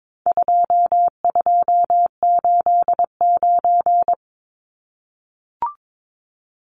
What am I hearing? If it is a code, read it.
Morse code, heard as a keyed tone, about 22 wpm, 700 Hz. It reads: 2289
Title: Numbers - 4-Digit 22wpm